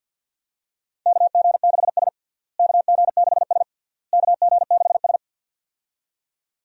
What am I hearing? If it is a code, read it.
Morse code, heard as a keyed tone, about 48 wpm, 700 Hz. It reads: KG6S KG6S KG6S
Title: USA Call Signs - 4 Characters - Repeated 3x - 50wpm